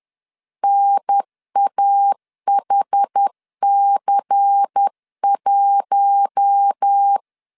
Morse code, heard as a keyed tone, about 10 wpm, 800 Hz. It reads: NAHC1
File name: LEZIONI / 6 PKX0 / 6 PKX0 10wpm 2707